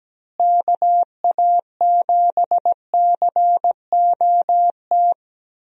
Morse code, heard as a keyed tone, about 17 wpm, 700 Hz. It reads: KA7COT
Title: USA Call Signs - 6 Characters - Repeated 2x - 17wpm